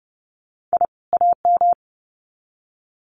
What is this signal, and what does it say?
Morse code, heard as a keyed tone, about 30 wpm, 700 Hz. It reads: I AM
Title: Sets of 2 Words 30wpm